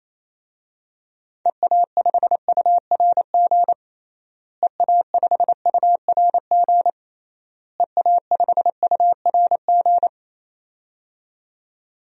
Morse code, heard as a keyed tone, about 28 wpm, 700 Hz. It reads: EA5URG EA5URG EA5URG
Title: DXCC-100 Call Signs - 6 Characters - Repeated 3x - 28wpm